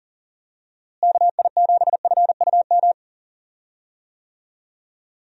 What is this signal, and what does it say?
Morse code, heard as a keyed tone, about 38 wpm, 700 Hz. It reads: KI7FUM
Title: USA Call Signs - 6 Characters - Repeated 2x - 40wpm